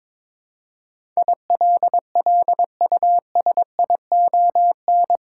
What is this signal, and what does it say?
Morse code, heard as a keyed tone, about 22 wpm, 700 Hz. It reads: ILLUSION
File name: practice / 22wpm / Eight Letter Words - 22wpm